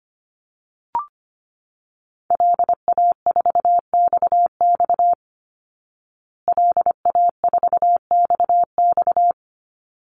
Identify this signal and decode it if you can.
Morse code, heard as a keyed tone, about 25 wpm, 700 Hz. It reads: LA4XX LA4XX
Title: Call Signs Repeated 3x - 25wpm